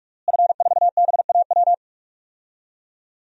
Morse code, heard as a keyed tone, about 46 wpm, 700 Hz. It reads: F4BAW
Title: DXCC-100 Call Signs - 5 Characters - Repeated 3x - 45wpm